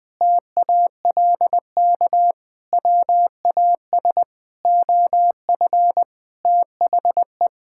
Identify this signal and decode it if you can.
Morse code, heard as a keyed tone, about 20 wpm, 700 Hz. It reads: TALK WAS OF THE
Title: Sentences from Top 1000 Words - Review Part 5 - 20wpm